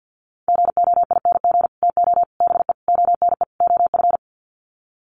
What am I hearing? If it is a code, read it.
Morse code, heard as a keyed tone, about 50 wpm, 700 Hz. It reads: GOING TO BE ONE OF
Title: Sets of 5 Words Encore2 50wpm